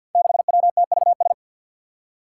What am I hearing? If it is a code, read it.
Morse code, heard as a keyed tone, about 52 wpm, 700 Hz. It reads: 6WTVS
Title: Five Challenging Letter-Number Combinations 50wpm